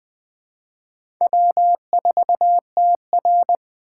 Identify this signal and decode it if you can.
Morse code, heard as a keyed tone, about 21 wpm, 700 Hz. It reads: W4TR
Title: USA Call Signs - 4 Characters - 4x Character Spacing - 20wpm